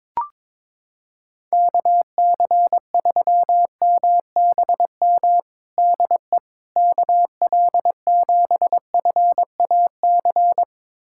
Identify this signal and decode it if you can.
Morse code, heard as a keyed tone, about 22 wpm, 700 Hz. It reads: KC3MBM DE KL7FAC
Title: USA Callsign DE Callsign 22wpm